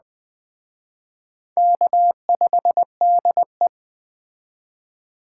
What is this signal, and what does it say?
Morse code, heard as a keyed tone, about 20 wpm, 700 Hz. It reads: K5DE
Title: Speed Racing - USA 1x2 Call Signs 17wpm